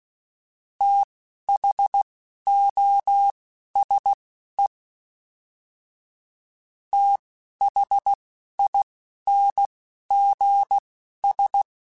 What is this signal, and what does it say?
Morse code, heard as a keyed tone, about 16 wpm, spaced as for 11 wpm, 800 Hz. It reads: THOSE THINGS
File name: TH Runs / TH Runs mp3s / TH Runs @10